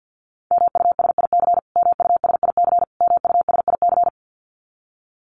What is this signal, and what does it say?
Morse code, heard as a keyed tone, about 50 wpm, 700 Hz. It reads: MUSIC MUSIC MUSIC
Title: Top 500 Words Repeated 50wpm